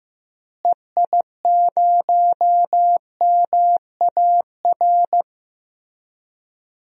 Morse code, heard as a keyed tone, about 15 wpm, 700 Hz. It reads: EI0MAR
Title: DXCC-100 Call Signs - 6 Characters - Repeated 3x - 15wpm